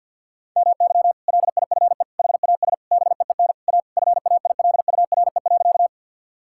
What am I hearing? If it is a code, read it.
Morse code, heard as a keyed tone, about 47 wpm, 700 Hz. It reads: MY LIFE HAS BEEN A FAILURE.
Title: Sentences from Top 1000 Words - Review Part 6 - 50wpm